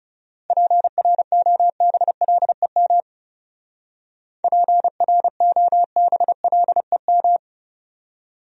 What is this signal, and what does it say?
Morse code, heard as a keyed tone, about 35 wpm, 700 Hz. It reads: PROBLEM PROBLEM
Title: Speed-Racing Top 500 Words 30wpm